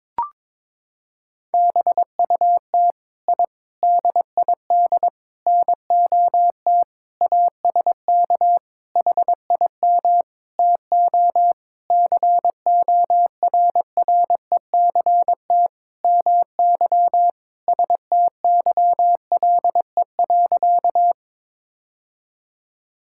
Morse code, heard as a keyed tone, about 21 wpm, 700 Hz. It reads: BUT I DID NOT ASK HIM TO CORRECT MY STYLE.
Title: Sentences from Top 1000 Words - Review Part 5 - 22wpm